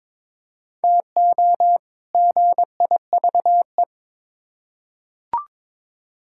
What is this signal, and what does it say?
Morse code, heard as a keyed tone, about 22 wpm, 700 Hz. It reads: TO GIVE
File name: Build A Phrase - 2-Word Phrases - 22wpm